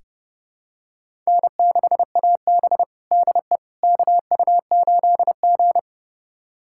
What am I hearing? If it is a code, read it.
Morse code, heard as a keyed tone, about 30 wpm, 700 Hz. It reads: N6AB DE KU8G
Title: USA Callsign DE Callsign - 4 Characters - 30wpm